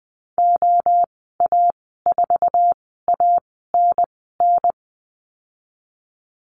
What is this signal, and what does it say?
Morse code, heard as a keyed tone, about 20 wpm, spaced as for 14 wpm, 700 Hz. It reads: OA4ANN
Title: 069 Lesson 12g - Letter d - Call Signs 20wpm